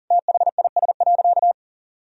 Morse code, heard as a keyed tone, about 38 wpm, 700 Hz. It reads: THIS.
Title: Sentences from Top 1000 Words - Review Part 2 - 40wpm